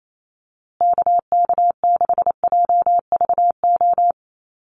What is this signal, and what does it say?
Morse code, heard as a keyed tone, about 28 wpm, 700 Hz. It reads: KK6JVO
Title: USA Call Signs Repeated 3x 28wpm